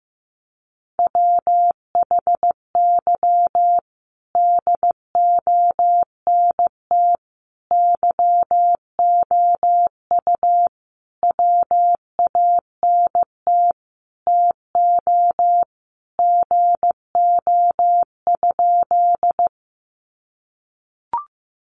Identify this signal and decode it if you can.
Morse code, heard as a keyed tone, about 15 wpm, 700 Hz. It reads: WHY DONT YOU WANT TO GO?
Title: Sentences from Top 300 Words 15wpm